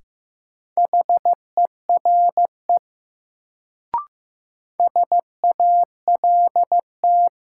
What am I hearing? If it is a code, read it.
Morse code, heard as a keyed tone, about 15 wpm, 700 Hz. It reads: HERE SALT
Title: Sets of 5 Words - New Word Drill - 15wpm